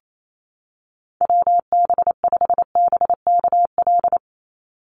Morse code, heard as a keyed tone, about 28 wpm, 700 Hz. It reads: WB5BKL
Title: NA SOTA Chaser Callsigns 28wpm